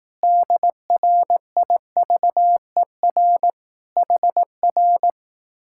Morse code, heard as a keyed tone, about 18 wpm, 700 Hz. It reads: DRIVER HR
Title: Speed-Racing Common QSO Phrases Encore - 15wpm